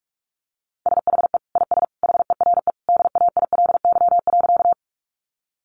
Morse code, heard as a keyed tone, about 44 wpm, 700 Hz. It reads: SHE IS HERE DAILY.
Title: Sentences from Top 1100 Words 45wpm